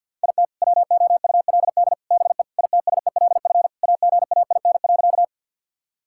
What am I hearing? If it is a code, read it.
Morse code, heard as a keyed tone, about 51 wpm, 700 Hz. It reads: IT WOULD BE ITSELF AGAIN.
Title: Sentences from Top 1000 Words - Review Part 5 - 50wpm